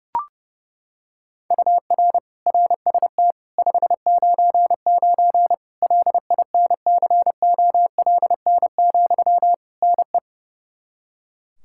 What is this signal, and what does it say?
Morse code, heard as a keyed tone, about 30 wpm, 700 Hz. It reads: UR RST 599 LINCOLN, NE
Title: Practice QSO - Exchange 1 - Beginning - 30wpm